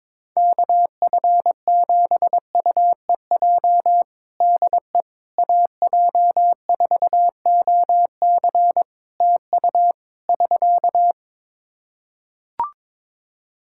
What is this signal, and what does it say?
Morse code, heard as a keyed tone, about 21 wpm, 700 Hz. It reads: KF7UEJ DE AJ4OC TU <SK>
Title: Practice QSO - Exchange 4 - Ending - 22wpm